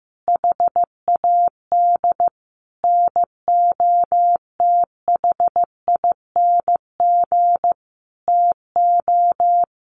Morse code, heard as a keyed tone, about 14 wpm, 700 Hz. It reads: HAD NOTHING TO
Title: Sentences from Top 1100 Words 15wpm